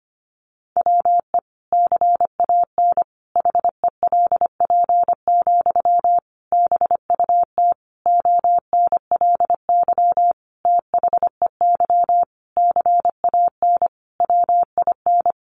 Morse code, heard as a keyed tone, about 25 wpm, 700 Hz. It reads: WE CAN HELP, BUT ONLY THEY CAN WIN
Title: Sentences from Top 1500 Words 25wpm